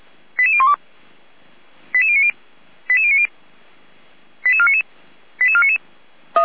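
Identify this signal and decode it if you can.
Five-tone selective call (ZVEI), heard as quick runs of five tones, tones of about 70 ms. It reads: 88012 88090 88090 88499 88499
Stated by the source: ZVEI five-tone call received on SDR, recorded by Feuerwehr Orlamünde.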